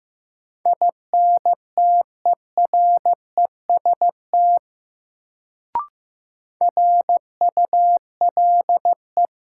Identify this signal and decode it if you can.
Morse code, heard as a keyed tone, about 15 wpm, 700 Hz. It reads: INTEREST RULE
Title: Sentences from Top 400 Words - New Word Drill - 15wpm